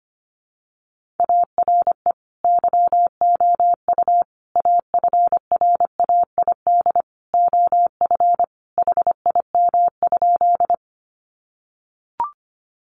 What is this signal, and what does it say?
Morse code, heard as a keyed tone, about 25 wpm, 700 Hz. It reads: ARE YOU AFRAID OF HIM?
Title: Sentences from Top 1500 Words 25wpm